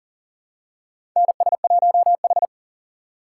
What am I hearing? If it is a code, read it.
Morse code, heard as a keyed tone, about 40 wpm, 700 Hz. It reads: NS1H
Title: USA Call Signs - 4 Characters - 6x Character Spacing - 40wpm